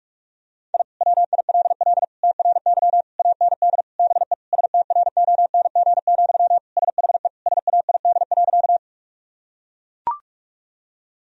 Morse code, heard as a keyed tone, about 41 wpm, 700 Hz. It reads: I WILL TRY AND BE STRONG, SHE SAID.
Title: Sentences from Top 1000 Words - Review Part 3 - 45wpm